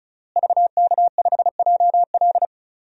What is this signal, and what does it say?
Morse code, heard as a keyed tone, about 34 wpm, 700 Hz. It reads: VK5JL
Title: DXCC-100 Call Signs - 5 Characters - Repeated 3x - 35wpm